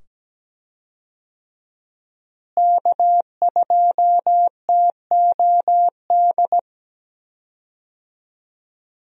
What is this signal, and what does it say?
Morse code, heard as a keyed tone, about 17 wpm, 700 Hz. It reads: K2TOD
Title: USA Call Signs - 5 Characters - Repeated 2x - 17wpm